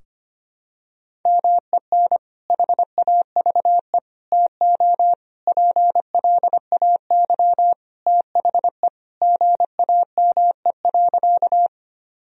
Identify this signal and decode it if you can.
Morse code, heard as a keyed tone, about 25 wpm, 700 Hz. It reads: MEN HAVE TO PLAY THE GAME.
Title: Sentences from Top 1000 Words - Review Part 3 - 25wpm